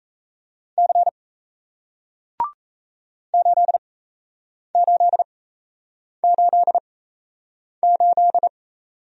Morse code, heard as a keyed tone, about 38 wpm, spaced as for 8 wpm, 700 Hz. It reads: C 8888